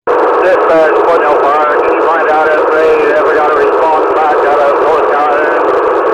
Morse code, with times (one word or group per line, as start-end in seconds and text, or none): none